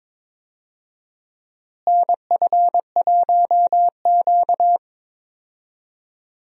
Morse code, heard as a keyed tone, about 22 wpm, 700 Hz. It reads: NF1Q
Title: USA Call Signs - 4 Characters - Repeated 3x - 22wpm